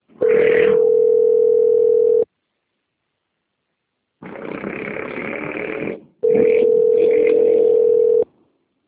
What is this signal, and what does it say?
Ringback tone (North America), 440 + 480 Hz.
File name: out- - - - - .85660